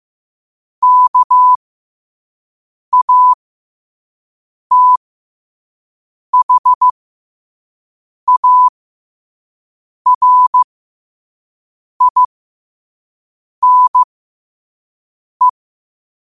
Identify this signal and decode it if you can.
Morse code, heard as a keyed tone, about 15 wpm, spaced as for 5 wpm, 1000 Hz. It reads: KATHARINE
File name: qotdfiles / 5wpm